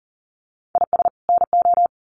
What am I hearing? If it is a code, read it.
Morse code, heard as a keyed tone, about 40 wpm, 700 Hz. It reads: IS NO
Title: Sets of 2 Words 40wpm